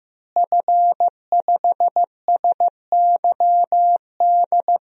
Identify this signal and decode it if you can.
Morse code, heard as a keyed tone, about 15 wpm, 700 Hz. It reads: F5SYD
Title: DXCC-100 Call Signs - 5 Characters - Repeated 2x - 15wpm